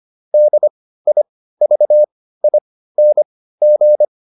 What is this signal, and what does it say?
Morse code, heard as a keyed tone, about 25 wpm, spaced as for 15 wpm, 600 Hz. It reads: DIVING